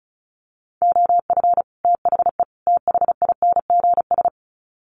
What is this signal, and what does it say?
Morse code, heard as a keyed tone, about 34 wpm, 700 Hz. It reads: OF THE THINGS
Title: Sets of 3 Words 35wpm